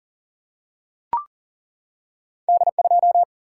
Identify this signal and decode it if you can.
Morse code, heard as a keyed tone, about 39 wpm, 700 Hz. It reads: D2
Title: Single-Letter Single-Number 40wpm